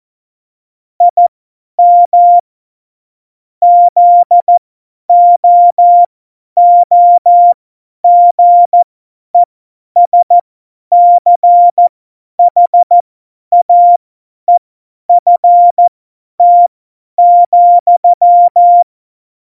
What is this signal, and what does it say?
Morse code, heard as a keyed tone, about 14 wpm, spaced as for 10 wpm, 700 Hz. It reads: IM ZOOGESCHAEFT,
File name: Morsekurs / Witze / Witz 06. November 2018, 50 F70